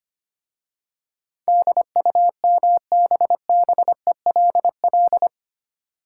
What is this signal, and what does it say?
Morse code, heard as a keyed tone, about 25 wpm, 700 Hz. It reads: DUMBBELL
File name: practice / 25wpm / Easy Words with Hints - 25wpm